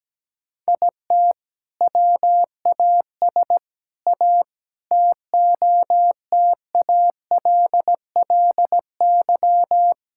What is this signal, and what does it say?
Morse code, heard as a keyed tone, about 17 wpm, 700 Hz. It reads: IT WAS A TOTALLY
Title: Sentences from Top 1600 Words 17wpm